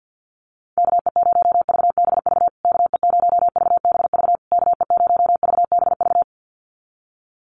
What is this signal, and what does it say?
Morse code, heard as a keyed tone, about 50 wpm, 700 Hz. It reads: KE0VBV KE0VBV KE0VBV
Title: USA Call Signs Repeated 3x 50wpm